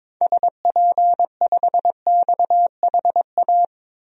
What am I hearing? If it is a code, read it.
Morse code, heard as a keyed tone, about 22 wpm, 700 Hz. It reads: SP5XHA
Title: DXCC-100 Call Signs - 6 Characters - Repeated 2x - 22wpm